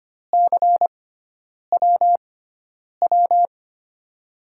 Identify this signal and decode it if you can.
Morse code, heard as a keyed tone, about 25 wpm, spaced as for 9 wpm, 700 Hz. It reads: CWW
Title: USA Call Signs - 6x Character Spacing - 25wpm